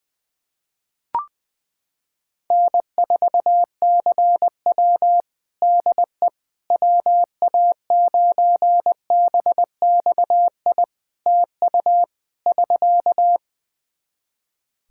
Morse code, heard as a keyed tone, about 20 wpm, 700 Hz. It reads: N4CW DE WA9BXI TU <SK>
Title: Practice QSO - Exchange 4 - Ending - 20wpm